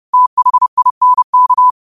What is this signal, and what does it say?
Morse code, heard as a keyed tone, about 30 wpm, 1000 Hz. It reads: THINK
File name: qotdfiles / 30wpm